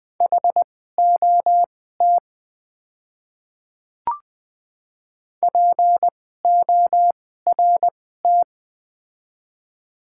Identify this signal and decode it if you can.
Morse code, heard as a keyed tone, about 20 wpm, spaced as for 14 wpm, 700 Hz. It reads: HOT PORT
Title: 2x Character Spacing - Top 200 Words - 20wpm